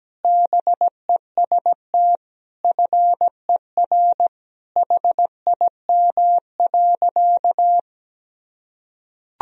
Morse code, heard as a keyed tone, about 17 wpm, 700 Hz. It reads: BEST FER HIM.
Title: Sentences from Top 500 Words and Abbreviations 17wpm